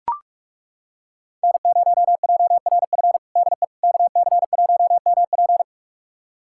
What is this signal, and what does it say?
Morse code, heard as a keyed tone, about 41 wpm, 700 Hz. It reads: N0JRF DE KC1MP
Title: USA Callsign DE Callsign - 5 Characters - 45wpm